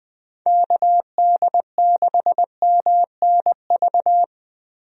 Morse code, heard as a keyed tone, about 21 wpm, 700 Hz. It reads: KD6MNV
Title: USA Call Signs - 6 Characters - 20wpm